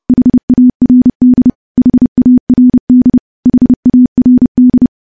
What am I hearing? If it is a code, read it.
Morse code, heard as a keyed tone, about 30 wpm, 250 Hz. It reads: HARD HARD HARD
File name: morse / morse-training-30WPM